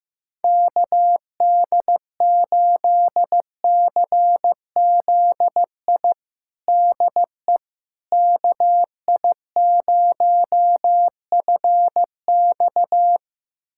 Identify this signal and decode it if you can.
Morse code, heard as a keyed tone, about 15 wpm, 700 Hz. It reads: KD8CZI DE KI0FX
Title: USA Callsign DE Callsign 15wpm